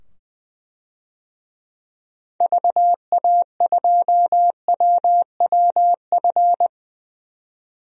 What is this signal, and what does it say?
Morse code, heard as a keyed tone, about 21 wpm, 700 Hz. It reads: VA2WWF
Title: DXCC-100 Call Signs - 6 Characters - 20wpm